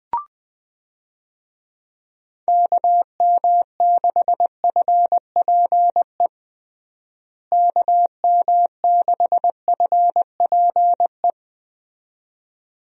Morse code, heard as a keyed tone, about 20 wpm, 700 Hz. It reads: KM6FPE KM6FPE
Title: USA Call Signs - 6 Characters - Repeated 2x - 20wpm